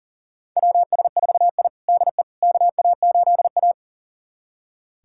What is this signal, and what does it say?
Morse code, heard as a keyed tone, about 40 wpm, 700 Hz. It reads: WS4I DE KA8A
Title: USA Callsign DE Callsign - 4 Characters - 40wpm